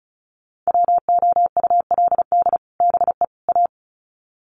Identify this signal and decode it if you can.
Morse code, heard as a keyed tone, about 34 wpm, 700 Hz. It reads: WOULD BE A
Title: Sets of 3 Words 35wpm